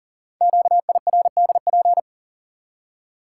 Morse code, heard as a keyed tone, about 43 wpm, 700 Hz. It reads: QIRDP
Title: Code Groups - 5 Letter:Numbers - 40wpm